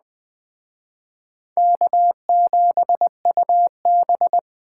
Morse code, heard as a keyed tone, about 20 wpm, 700 Hz. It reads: K7UB
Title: Speed Racing - USA Call Signs 17wpm